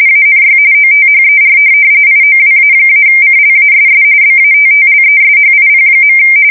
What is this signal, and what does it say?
RTTY, about 45.45 baud. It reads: ANY OTHER IOWA-CLASS BATTLESHIP.
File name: fsk170 / USS-New-Jersey